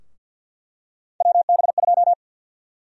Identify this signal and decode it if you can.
Morse code, heard as a keyed tone, about 47 wpm, 700 Hz. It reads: WB2
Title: USA Call Sign Prefixes - 50wpm